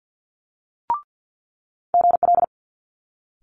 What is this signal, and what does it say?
Morse code, heard as a keyed tone, about 50 wpm, 700 Hz. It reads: GL
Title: Multi-letter Phonemes Quick Fire 50wpm